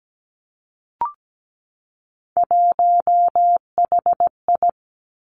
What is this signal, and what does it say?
Morse code, heard as a keyed tone, about 17 wpm, 700 Hz. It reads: 1HI
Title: Three Challenging Letter-Number Combinations 17wpm